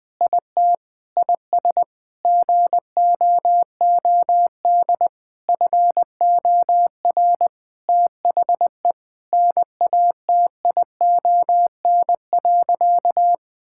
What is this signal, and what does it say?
Morse code, heard as a keyed tone, about 20 wpm, 700 Hz. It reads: IT IS GOOD FOR THE NATION.
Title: Sentences with 4-Word Phrases - Spoken Beginning - 20wpm